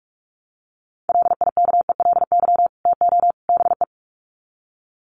Morse code, heard as a keyed tone, about 44 wpm, 700 Hz. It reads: LIKELY TO BE
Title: Sets of 3 Words 45wpm